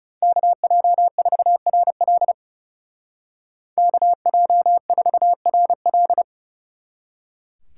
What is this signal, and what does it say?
Morse code, heard as a keyed tone, about 40 wpm, 700 Hz. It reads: KJ4RL KJ4RL